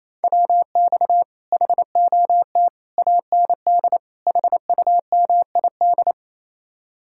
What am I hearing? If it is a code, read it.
Morse code, heard as a keyed tone, about 34 wpm, 700 Hz. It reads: WX HOT AND HUMID
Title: Speed-Racing Common QSO Phrases - 28wpm